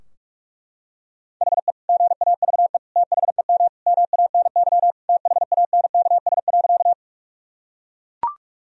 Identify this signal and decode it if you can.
Morse code, heard as a keyed tone, about 42 wpm, 700 Hz. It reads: HE GAVE THEM MANY THANKS.
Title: Sentences from Top 1000 Words - Review Part 3 - 45wpm